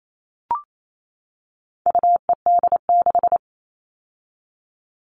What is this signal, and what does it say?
Morse code, heard as a keyed tone, about 28 wpm, 700 Hz. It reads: UED6
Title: All Cut Numbers - 4-Digit 28wpm